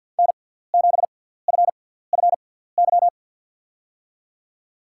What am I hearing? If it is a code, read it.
Morse code, heard as a keyed tone, about 53 wpm, spaced as for 17 wpm, 700 Hz. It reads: N7FFY
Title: USA Call Signs - 5 Characters - 6x Character Spacing - 50wpm